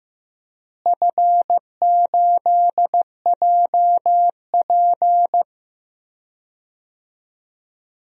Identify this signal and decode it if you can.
Morse code, heard as a keyed tone, about 15 wpm, 700 Hz. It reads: F8JP
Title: DXCC-100 Call Signs - 4 Characters - Repeated 2x - 15wpm